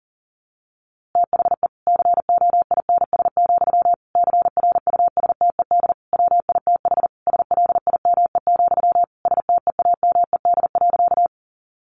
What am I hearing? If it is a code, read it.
Morse code, heard as a keyed tone, about 40 wpm, 700 Hz. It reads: THE COINS, CRUSTED WITH SLIME, STEAMED.
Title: A Sound Of Thunder 40wpm